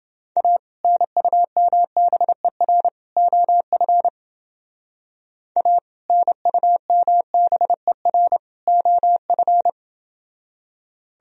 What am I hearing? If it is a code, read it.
Morse code, heard as a keyed tone, about 29 wpm, 700 Hz. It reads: A NUMBER OF A NUMBER OF